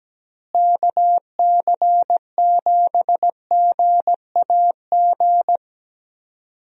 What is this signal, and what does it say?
Morse code, heard as a keyed tone, about 17 wpm, 700 Hz. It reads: KC7GAG
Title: USA Call Signs - 6 Characters - Repeated 2x - 17wpm